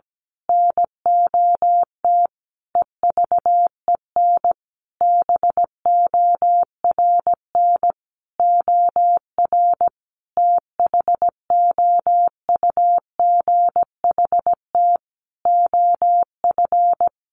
Morse code, heard as a keyed tone, about 17 wpm, 700 Hz. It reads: NOT EVEN BORN OR THOUGHT OF
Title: A Sound Of Thunder 17wpm